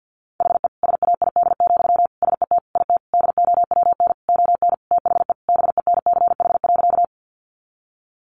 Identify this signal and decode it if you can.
Morse code, heard as a keyed tone, about 50 wpm, 700 Hz. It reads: HE SAID, SET IT DOWN ON THE BENCH.
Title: Sentences from Top 1400 Words 50wpm